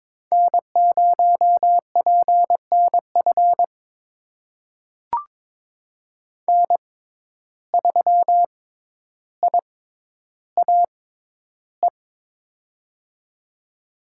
Morse code, heard as a keyed tone, about 21 wpm, spaced as for 11 wpm, 700 Hz. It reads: N0PNF N3IAE
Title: USA Call Signs - 5 Characters - 6x Character Spacing - 22wpm